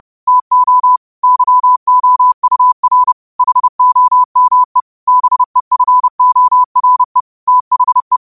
Morse code, heard as a keyed tone, about 30 wpm, 1000 Hz. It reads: TO YOUR HOME BEFORE THE
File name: qotdfiles / 30wpm